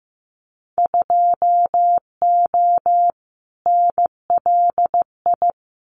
Morse code, heard as a keyed tone, about 15 wpm, 700 Hz. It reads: 2O NLI
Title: Winter Field Day Exchanges 15wpm